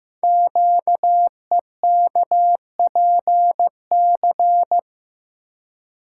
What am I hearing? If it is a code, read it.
Morse code, heard as a keyed tone, about 15 wpm, 700 Hz. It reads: QEKPC
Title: Code Groups - 5 Letters - 15wpm